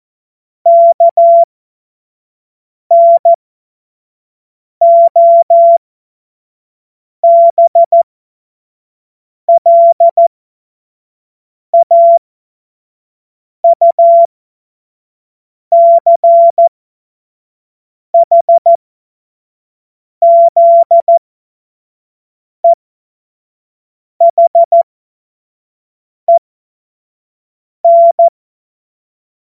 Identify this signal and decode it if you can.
Morse code, heard as a keyed tone, about 14 wpm, spaced as for 5 wpm, 680 Hz. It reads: KNOBLAUCHZEHEN
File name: Morsekurs / Witze / Witz 03. August 2021, 25 F70